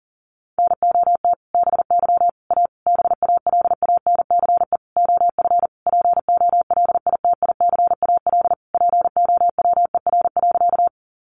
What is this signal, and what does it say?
Morse code, heard as a keyed tone, about 37 wpm, 700 Hz. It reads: NOT BY A BALANCE OF POLITICAL POWER.
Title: Sentences from Top 900 Words 40wpm